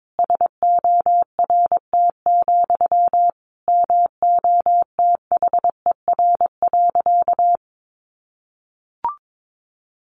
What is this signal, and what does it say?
Morse code, heard as a keyed tone, about 22 wpm, 700 Hz. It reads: SORT, MOTHER.
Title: Sentences from Top 1300 Words 22wpm